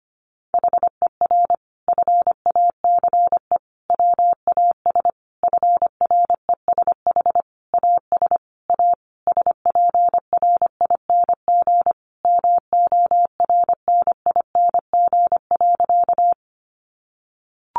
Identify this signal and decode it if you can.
Morse code, heard as a keyed tone, about 25 wpm, 700 Hz. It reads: HER FACE WAS FRESH AS A SPRING MORNING.
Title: Sentences from Top 1200 Words 25wpm